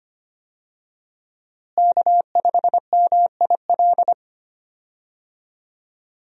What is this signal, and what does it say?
Morse code, heard as a keyed tone, about 25 wpm, 700 Hz. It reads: K5MIL
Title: USA Call Signs - 5 Characters - Repeated 2x - 25wpm